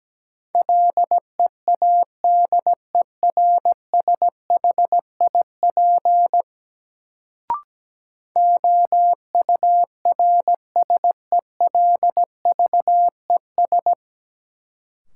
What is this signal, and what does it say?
Morse code, heard as a keyed tone, about 17 wpm, 700 Hz. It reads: LEADERSHIP OURSELVES
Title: Sentences from Top 1600 Words - New Word Drill Alt - 17wpm